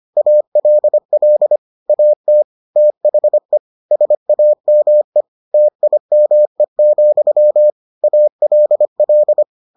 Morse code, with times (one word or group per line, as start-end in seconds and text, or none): 0.2-1.6 ALL
1.9-2.4 AT
2.8-3.6 THE
3.9-5.2 SAME
5.5-7.7 TIME,
8.0-9.4 ALL